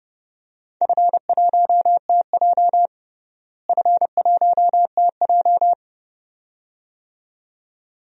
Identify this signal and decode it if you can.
Morse code, heard as a keyed tone, about 30 wpm, 700 Hz. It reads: F1TJ F1TJ
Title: DXCC-100 Call Signs - 4 Characters - Repeated 2x - 30wpm